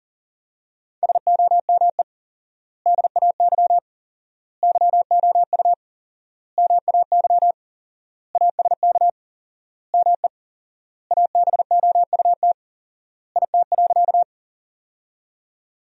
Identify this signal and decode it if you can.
Morse code, heard as a keyed tone, about 37 wpm, 700 Hz. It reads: SOME DAY YOU MAY ASK ME ABOUT IT.
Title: 4x Word-Spacing - Sentences from Top 200 Words - 40wpm